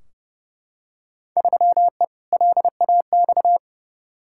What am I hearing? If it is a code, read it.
Morse code, heard as a keyed tone, about 30 wpm, 700 Hz. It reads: 3E LAX
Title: Field Day Exchanges Encore - 30wpm